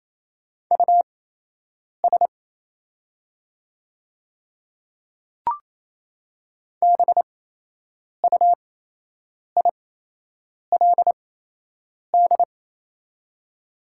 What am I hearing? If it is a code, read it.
Morse code, heard as a keyed tone, about 28 wpm, spaced as for 8 wpm, 700 Hz. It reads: US BUILD
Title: 8x Character Spacing - Top 200 Words - 28wpm